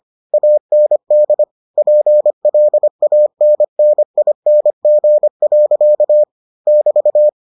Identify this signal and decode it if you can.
Morse code, heard as a keyed tone, about 25 wpm, 600 Hz. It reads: AND PLANNING. <BT>